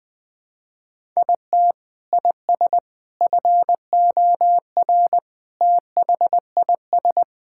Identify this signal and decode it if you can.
Morse code, heard as a keyed tone, about 20 wpm, 700 Hz. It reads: IT IS FOR THIS
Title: Sentences with 6-Word Phrases - 20wpm